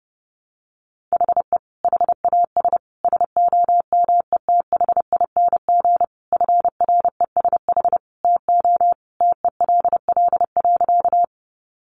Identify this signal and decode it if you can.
Morse code, heard as a keyed tone, about 30 wpm, 700 Hz. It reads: HE HAS SOMETHING FRESH TO TELL.
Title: Sentences from Top 1200 Words 30wpm